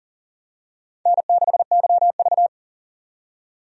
Morse code, heard as a keyed tone, about 38 wpm, 700 Hz. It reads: N6YV
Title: USA Call Signs - 4 Characters - 40wpm